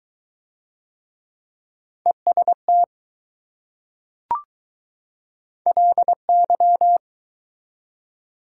Morse code, heard as a keyed tone, about 20 wpm, 700 Hz. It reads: EST LY